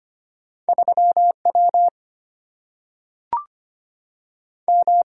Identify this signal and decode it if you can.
Morse code, heard as a keyed tone, about 25 wpm, 700 Hz. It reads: 3 W M
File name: Two Letter-Number Combinations - 25wpm